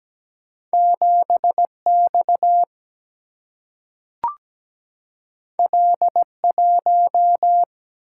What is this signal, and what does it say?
Morse code, heard as a keyed tone, about 17 wpm, 700 Hz. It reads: 7X L1
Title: Single-Letter Single-Number 17wpm